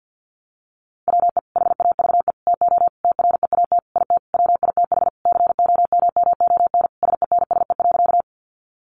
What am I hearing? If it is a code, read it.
Morse code, heard as a keyed tone, about 50 wpm, 700 Hz. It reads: WE HAVE TO TREAT IT WITH COMMON SENSE.
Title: Sentences from Top 1300 Words 50wpm